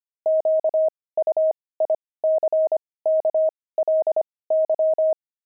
Morse code, heard as a keyed tone, about 25 wpm, spaced as for 18 wpm, 630 Hz. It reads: QUICKLY
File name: pangram-fs-18-cs-25